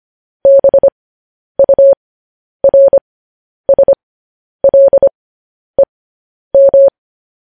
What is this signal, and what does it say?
Morse code, heard as a keyed tone, about 25 wpm, spaced as for 10 wpm, 550 Hz. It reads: BURSLEM
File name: podcast / News headlines-1x-25x10